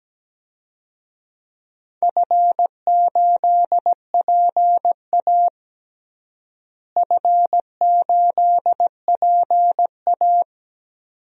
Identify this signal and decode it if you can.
Morse code, heard as a keyed tone, about 17 wpm, 700 Hz. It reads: F8PA F8PA
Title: DXCC-100 Call Signs - 4 Characters - Repeated 2x - 17wpm